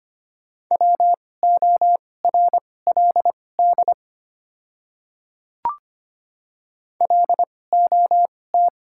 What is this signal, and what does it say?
Morse code, heard as a keyed tone, about 25 wpm, spaced as for 18 wpm, 700 Hz. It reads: WORLD LOT
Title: 2x Character Spacing - Top 500 Words - 25wpm